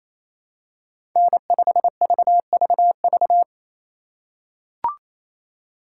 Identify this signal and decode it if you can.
Morse code, heard as a keyed tone, about 28 wpm, 700 Hz. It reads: N5VVV
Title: USA Call Signs - 5 Characters - 4x Character Spacing - 28wpm